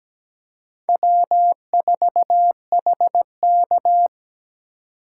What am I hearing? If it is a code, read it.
Morse code, heard as a keyed tone, about 17 wpm, 700 Hz. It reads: W4HK
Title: USA Call Signs - 4 Characters - 4x Character Spacing - 17wpm